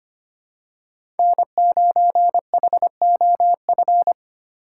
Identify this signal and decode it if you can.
Morse code, heard as a keyed tone, about 25 wpm, 700 Hz. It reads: N9HOF
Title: USA Call Signs - 5 Characters - 4x Character Spacing - 25wpm